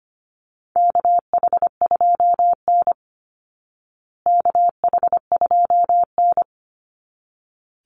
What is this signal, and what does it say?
Morse code, heard as a keyed tone, about 25 wpm, 700 Hz. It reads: KH2N KH2N
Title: Call Signs Repeated 3x - 25wpm